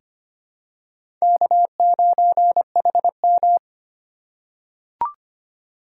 Morse code, heard as a keyed tone, about 25 wpm, 700 Hz. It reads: K9HM
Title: USA Call Signs - 3x Character Spacing - 25wpm